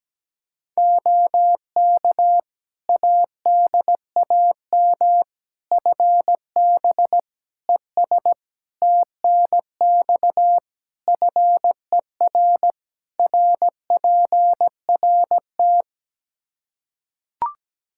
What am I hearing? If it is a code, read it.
Morse code, heard as a keyed tone, about 17 wpm, 700 Hz. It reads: OK ADAM FB ES TNX FER RPRT
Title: Practice QSO - Exchange 2 - Info - 17wpm